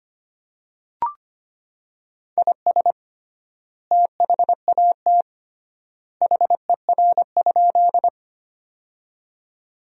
Morse code, heard as a keyed tone, about 25 wpm, 700 Hz. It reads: IS THAT HER?
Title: 3x Word-Spacing - Sentences from Top 100 Words - 25wpm